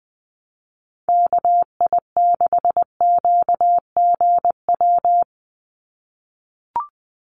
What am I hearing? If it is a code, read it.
Morse code, heard as a keyed tone, about 20 wpm, 700 Hz. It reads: KI6QGW
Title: USA Call Signs 20wpm